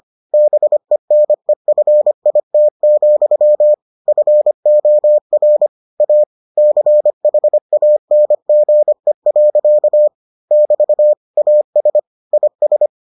Morse code, heard as a keyed tone, about 25 wpm, 600 Hz. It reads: BENEFIT, FOR A CHANGE. <BT> AS IS